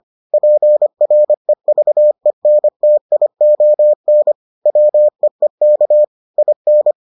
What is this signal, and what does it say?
Morse code, heard as a keyed tone, about 25 wpm, 600 Hz. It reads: PREVENTION WEEK IN